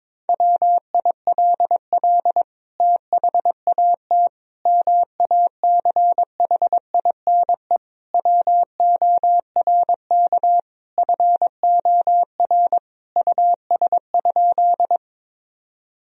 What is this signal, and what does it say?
Morse code, heard as a keyed tone, about 22 wpm, 700 Hz. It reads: WILL THAT MACHINE WORK FOR US?
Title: Sentences from Top 1000 Words - Review Part 2 - 22wpm